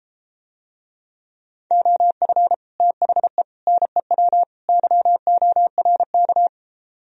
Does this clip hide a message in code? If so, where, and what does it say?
Yes: Morse code from 1.7 to 6.5 s, heard as a keyed tone, about 33 wpm, 700 Hz. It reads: OF THE NEW YORK